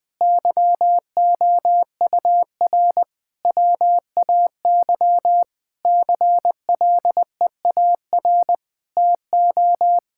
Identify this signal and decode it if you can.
Morse code, heard as a keyed tone, about 20 wpm, 700 Hz. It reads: YOUR WAY CLEAR TO
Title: Sets of 9 Words - 20wpm